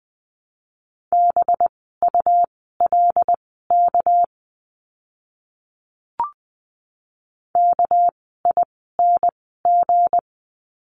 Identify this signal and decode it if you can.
Morse code, heard as a keyed tone, about 20 wpm, spaced as for 14 wpm, 700 Hz. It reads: BULK KING
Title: 199 Lesson 32f - Letter k - Words 20wpm